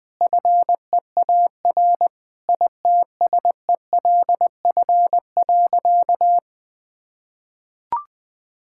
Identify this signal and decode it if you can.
Morse code, heard as a keyed tone, about 20 wpm, 700 Hz. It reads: FEAR ITSELF.
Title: Sentences with 9-Word Phrases - 20wpm